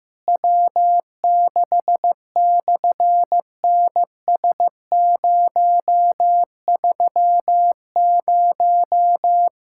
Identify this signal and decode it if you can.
Morse code, heard as a keyed tone, about 14 wpm, 700 Hz. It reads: W6/NS030
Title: SOTA QSO Practice 15wpm